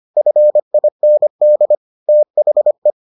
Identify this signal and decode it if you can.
Morse code, heard as a keyed tone, about 25 wpm, 600 Hz. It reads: FIND THE